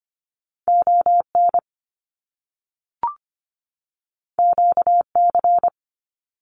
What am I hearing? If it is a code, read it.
Morse code, heard as a keyed tone, about 25 wpm, 700 Hz. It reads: ON QC
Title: Canadian Provinces and Territories 25wpm